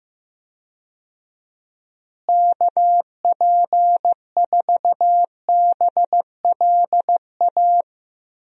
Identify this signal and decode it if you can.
Morse code, heard as a keyed tone, about 16 wpm, 700 Hz. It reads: KP4BLA
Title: DXCC-100 Call Signs - 6 Characters - Repeated 3x - 15wpm